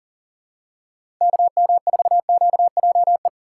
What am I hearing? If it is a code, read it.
Morse code, heard as a keyed tone, about 40 wpm, 690 Hz. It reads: KM4QJE
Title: USA Call Signs - 6 Characters - 2x Character Spacing - 40wpm